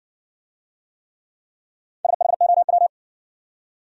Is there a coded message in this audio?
Yes: Morse code from 2.0 to 2.9 s, heard as a keyed tone, about 54 wpm, 700 Hz. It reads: SHOW